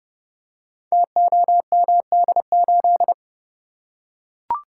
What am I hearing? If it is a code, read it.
Morse code, heard as a keyed tone, about 30 wpm, 700 Hz. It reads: TOMD8
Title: Code Groups - 5 Letter:Numbers - 30wpm